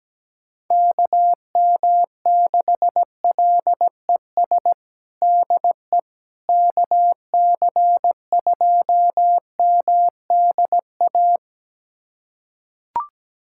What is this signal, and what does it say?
Morse code, heard as a keyed tone, about 17 wpm, 700 Hz. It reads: KM6LES DE KC2MDA
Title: USA Callsign DE Callsign - 6 Characters - 17wpm